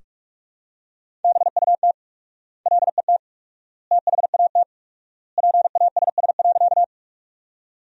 Morse code, heard as a keyed tone, about 46 wpm, 700 Hz. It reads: BUT LET THAT PASS.
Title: 4x Word-Spacing - Sentences from Top 400 Words - 45wpm